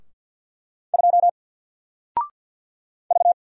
Morse code, heard as a keyed tone, about 50 wpm, 700 Hz. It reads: 2 V
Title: Single Character (Letter:Number:Punctuation:Prosign) 50wpm